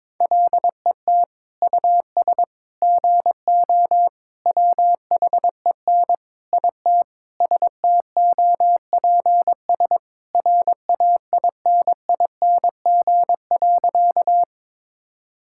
Morse code, heard as a keyed tone, about 22 wpm, 700 Hz. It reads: LET US GO WHEN IT STOPS RAINING.
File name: practice / 22wpm / Sentences with 2-Word Phrases - 22wpm